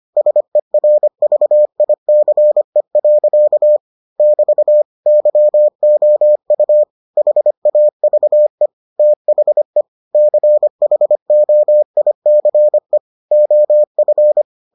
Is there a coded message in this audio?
Yes: Morse code from 0.2 to 14.4 s, heard as a keyed tone, about 25 wpm, 600 Hz. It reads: SERVICE. <BT> YOU HAVE THE CHOICE OF